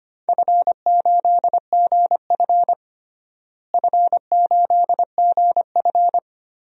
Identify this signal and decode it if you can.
Morse code, heard as a keyed tone, about 25 wpm, 700 Hz. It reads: F8GF F8GF
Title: DXCC-100 Call Signs - 4 Characters - Repeated 3x - 25wpm